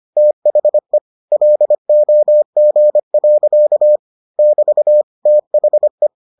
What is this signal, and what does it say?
Morse code, heard as a keyed tone, about 25 wpm, 600 Hz. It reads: THE LOG. <BT> THE